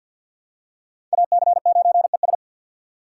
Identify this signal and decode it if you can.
Morse code, heard as a keyed tone, about 50 wpm, 700 Hz. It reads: AC9ES
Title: USA Call Signs - 5 Characters - 4x Character Spacing - 50wpm